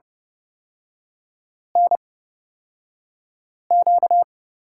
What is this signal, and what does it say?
Morse code, heard as a keyed tone, about 30 wpm, spaced as for 5 wpm, 700 Hz. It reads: NQ
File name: Lesson 38e - Letter q - Three Characters 30wpm